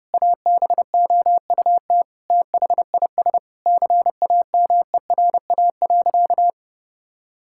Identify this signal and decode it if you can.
Morse code, heard as a keyed tone, about 32 wpm, 700 Hz. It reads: ABOUT THIS CAMERA.
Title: Sentences from Top 1000 Words - Review Part 5 - 30wpm